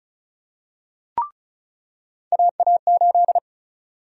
Morse code, heard as a keyed tone, about 34 wpm, 700 Hz. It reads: AA8
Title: USA Call Sign Prefixes - 35wpm